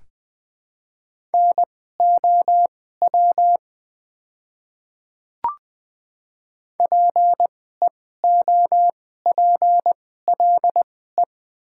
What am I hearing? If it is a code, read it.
Morse code, heard as a keyed tone, about 20 wpm, spaced as for 15 wpm, 700 Hz. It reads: NOW PEOPLE
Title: 2x Character Spacing - Top 100 Words - 20wpm